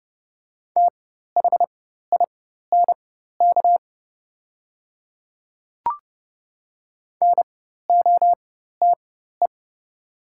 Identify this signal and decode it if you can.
Morse code, heard as a keyed tone, about 30 wpm, spaced as for 14 wpm, 700 Hz. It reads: THINK NOTE
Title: 4x Character Spacing - Top 500 Words - 30wpm